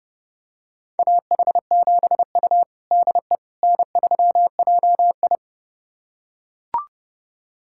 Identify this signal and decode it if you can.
Morse code, heard as a keyed tone, about 32 wpm, 700 Hz. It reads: AH7U DE N3JI
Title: USA Callsign DE Callsign - 4 Characters - 30wpm